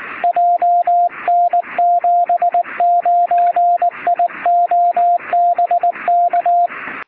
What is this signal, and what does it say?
Morse code, heard as a keyed tone, about 19 wpm, 700 Hz, with radio noise. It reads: JN79IO BK
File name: sm7lcb / 0922z ok1mac 23cm